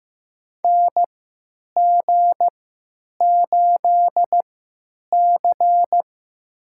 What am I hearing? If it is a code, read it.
Morse code, heard as a keyed tone, about 15 wpm, spaced as for 9 wpm, 700 Hz. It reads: NG8C
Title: USA Call Signs - 3x Character Spacing - 15wpm